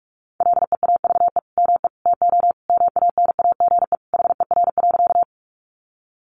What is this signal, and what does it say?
Morse code, heard as a keyed tone, about 44 wpm, 700 Hz. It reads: LEAVE ME TO MANAGE HER.
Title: Sentences from Top 1400 Words 45wpm